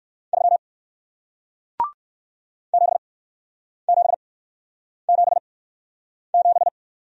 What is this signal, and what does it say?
Morse code, heard as a keyed tone, about 60 wpm, spaced as for 9 wpm, 700 Hz. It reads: 3 7777